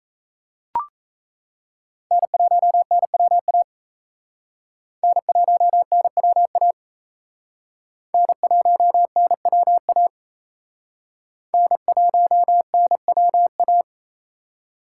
Morse code, heard as a keyed tone, about 35 wpm, 700 Hz. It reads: N1NWA N1NWA N1NWA N1NWA
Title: Speed Racing - USA Call Signs 28wpm